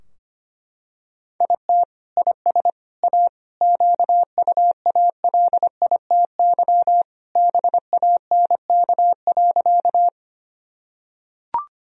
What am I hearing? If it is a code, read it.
Morse code, heard as a keyed tone, about 25 wpm, 700 Hz. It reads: IT IS A QUALITY BANK.
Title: Sentences from Top 1000 Words - Review Part 4 - 25wpm